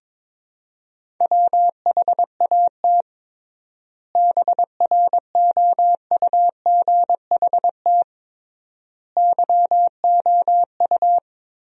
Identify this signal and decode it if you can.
Morse code, heard as a keyed tone, about 22 wpm, 700 Hz. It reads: WHAT BROUGHT YOU
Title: 3x Word-Spacing - Sentences from Top 500 Words - 22wpm